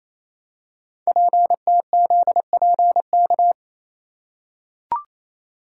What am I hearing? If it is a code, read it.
Morse code, heard as a keyed tone, about 27 wpm, 700 Hz. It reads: PTZPK
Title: Code Groups - 5 Letters - 28wpm